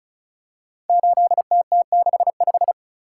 Morse code, heard as a keyed tone, about 36 wpm, 700 Hz. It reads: 8TT65
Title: Code Groups - 5 Letter:Numbers - 35wpm